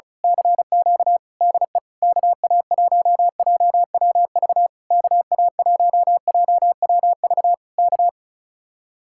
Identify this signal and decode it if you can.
Morse code, heard as a keyed tone, about 32 wpm, 700 Hz. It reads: CQ DE KA1JWV KA1JWV K
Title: Speed-Racing Common QSO Phrases Encore - 35wpm